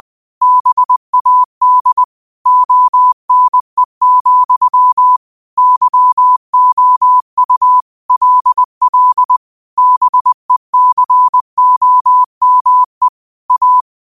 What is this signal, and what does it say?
Morse code, heard as a keyed tone, about 20 wpm, 1000 Hz. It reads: BAD ONE, YOU LL BECOME A